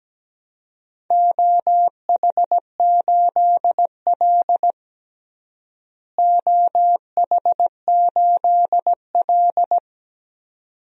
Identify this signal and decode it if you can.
Morse code, heard as a keyed tone, about 17 wpm, 700 Hz. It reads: OH8L OH8L
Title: DXCC-100 Call Signs - 4 Characters - Repeated 2x - 17wpm